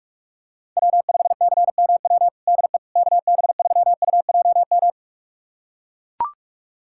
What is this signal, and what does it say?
Morse code, heard as a keyed tone, about 41 wpm, 700 Hz. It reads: W5CMW DE KB3UJM
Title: USA Callsign DE Callsign 45wpm